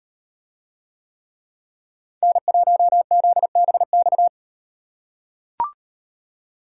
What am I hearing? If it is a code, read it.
Morse code, heard as a keyed tone, about 38 wpm, 700 Hz. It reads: N1ZBX